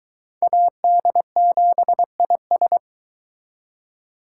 Morse code, heard as a keyed tone, about 23 wpm, 700 Hz. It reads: AD7IS